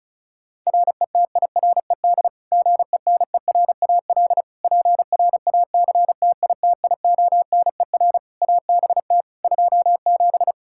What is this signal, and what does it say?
Morse code, heard as a keyed tone, about 36 wpm, 700 Hz. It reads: RETIRED GENERAL PRACTITIONER ABT 27
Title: Practice QSO - Exchange 3 - More Info - 35wpm